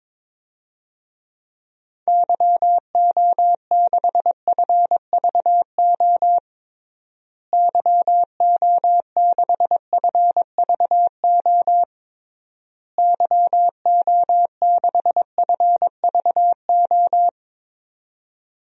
Morse code, heard as a keyed tone, about 22 wpm, 700 Hz. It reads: YO6FVO YO6FVO YO6FVO
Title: DXCC-100 Call Signs - 6 Characters - Repeated 3x - 22wpm